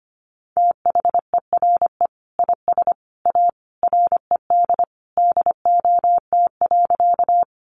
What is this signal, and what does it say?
Morse code, heard as a keyed tone, about 25 wpm, 700 Hz. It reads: THERE IS A RED DOT.
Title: Sentences from Top 1500 Words 25wpm